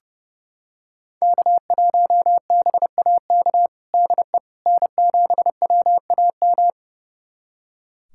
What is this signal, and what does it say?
Morse code, heard as a keyed tone, about 30 wpm, 700 Hz. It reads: K1BAK DE N7WAM
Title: USA Callsign DE Callsign - 5 Characters - 30wpm